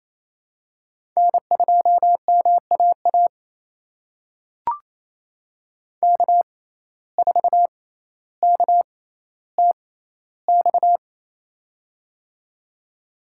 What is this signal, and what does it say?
Morse code, heard as a keyed tone, about 28 wpm, spaced as for 14 wpm, 700 Hz. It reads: N2MAA K4KTX
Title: USA Call Signs - 6x Character Spacing - 28wpm